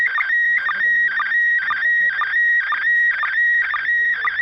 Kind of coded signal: SSTV (slow-scan television)